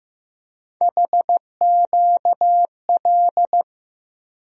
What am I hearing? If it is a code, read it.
Morse code, heard as a keyed tone, about 15 wpm, 700 Hz. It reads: HQL
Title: USA Call Sign Postfixes - 3 Letters 15wpm